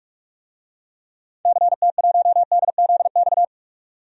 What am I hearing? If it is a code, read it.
Morse code, heard as a keyed tone, about 47 wpm, 700 Hz. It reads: CT1DZX
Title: DXCC-100 Call Signs - 6 Characters - 45wpm